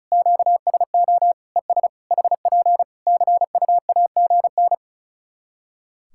Morse code, heard as a keyed tone, about 34 wpm, 700 Hz. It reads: QSO ES HP CUAGN
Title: Practice QSO - Exchange 4 - Ending - 35wpm